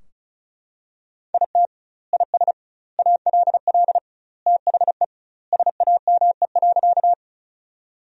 Morse code, heard as a keyed tone, about 36 wpm, 700 Hz. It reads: IT IS ALL THE SAME.
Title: 2x Word-Spacing - Sentences from Top 200 Words - 35wpm